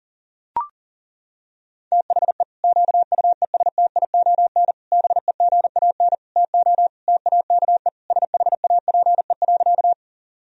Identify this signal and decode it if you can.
Morse code, heard as a keyed tone, about 40 wpm, 700 Hz. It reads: THE QUESTION BEGAN TO TAKE SHAPE.
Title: Sentences from Top 1000 Words - Review Part 3 - 40wpm